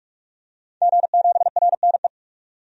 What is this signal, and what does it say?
Morse code, heard as a keyed tone, about 46 wpm, 700 Hz. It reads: G7RNE
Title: DXCC-100 Call Signs - 5 Characters - 45wpm